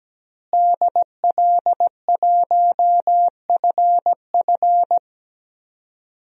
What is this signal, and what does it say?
Morse code, heard as a keyed tone, about 17 wpm, 700 Hz. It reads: DL1FF
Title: DXCC-100 Call Signs - 5 Characters - Repeated 2x - 17wpm